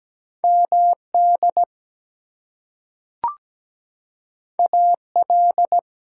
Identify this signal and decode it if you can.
Morse code, heard as a keyed tone, about 17 wpm, 700 Hz. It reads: MD AL
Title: US States and Canadian Provinces 17wpm